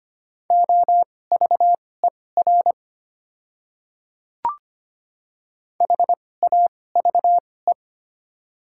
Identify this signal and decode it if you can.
Morse code, heard as a keyed tone, about 25 wpm, spaced as for 18 wpm, 700 Hz. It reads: OVER HAVE
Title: 2x Character Spacing - Top 100 Words - 25wpm